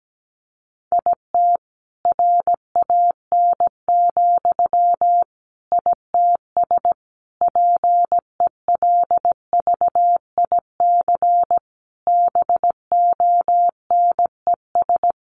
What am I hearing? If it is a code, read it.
Morse code, heard as a keyed tone, about 17 wpm, 700 Hz. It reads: IT RAN, ITS PELVIC BONES
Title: A Sound Of Thunder 17wpm